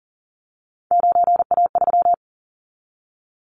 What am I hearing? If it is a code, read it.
Morse code, heard as a keyed tone, about 40 wpm, 700 Hz. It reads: 9A3
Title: Callsign Prefixes for Top 25 DXCC - 40wpm